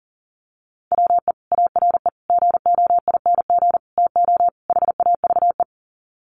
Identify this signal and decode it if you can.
Morse code, heard as a keyed tone, about 40 wpm, 700 Hz. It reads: WE ARE GOING TO HAVE
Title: Sets of 5 Words Encore 40wpm